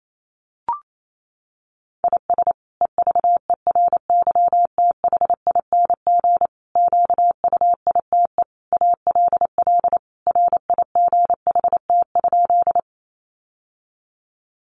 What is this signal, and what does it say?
Morse code, heard as a keyed tone, about 28 wpm, 700 Hz. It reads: IS EVERYTHING QUITE ALL RIGHT?
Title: Sentences from Top 700 Words 28wpm